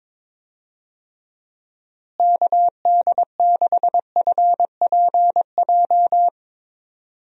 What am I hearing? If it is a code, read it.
Morse code, heard as a keyed tone, about 22 wpm, 700 Hz. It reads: KD6FPJ
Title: USA Call Signs - 6 Characters - Repeated 3x - 22wpm